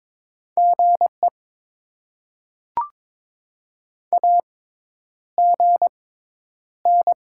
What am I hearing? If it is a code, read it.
Morse code, heard as a keyed tone, about 22 wpm, spaced as for 10 wpm, 700 Hz. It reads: GE AGN
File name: Top 20 QSO Elements - 6x Character Spacing - 22wpm